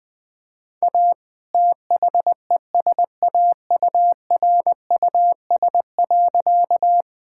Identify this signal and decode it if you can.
Morse code, heard as a keyed tone, about 21 wpm, 700 Hz. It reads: A THESAURUS.
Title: All-Age Appropriate Jokes - 20wpm